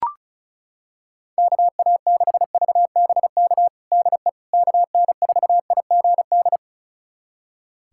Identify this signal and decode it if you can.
Morse code, heard as a keyed tone, about 35 wpm, 700 Hz. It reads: KA6VBK DE KN4IGD
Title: USA Callsign DE Callsign 35wpm